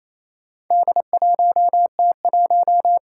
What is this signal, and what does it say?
Morse code, heard as a keyed tone, about 27 wpm, 700 Hz. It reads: D1T1
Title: Four Challenging Letter-Number Combinations 28wpm